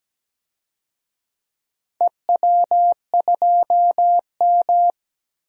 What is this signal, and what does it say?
Morse code, heard as a keyed tone, about 17 wpm, 700 Hz. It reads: EW2M
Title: DXCC-100 Call Signs - 4 Characters - 17wpm